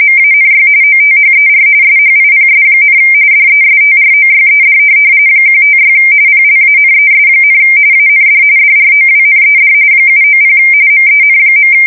RTTY, about 45.45 baud. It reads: MANY OTHER SHIPS, DESTROYED 156 PLANES, AND DROWNED 790 MEN.
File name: fsk170 / USS-New-Jersey